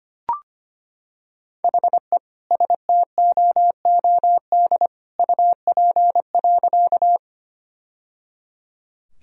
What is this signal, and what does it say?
Morse code, heard as a keyed tone, about 25 wpm, 700 Hz. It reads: HE STOOD UP.
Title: Sentences from Top 1000 Words - Review Part 2 - 25wpm